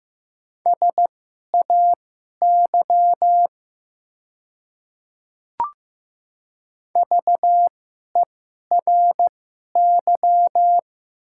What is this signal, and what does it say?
Morse code, heard as a keyed tone, about 15 wpm, spaced as for 11 wpm, 700 Hz. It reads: SAY VERY
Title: 2x Character Spacing - Top 300 Words - 15wpm